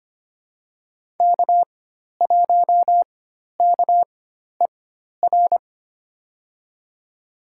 Morse code, heard as a keyed tone, about 25 wpm, spaced as for 12 wpm, 700 Hz. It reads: K1KER
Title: USA Call Signs - 4x Character Spacing - 25wpm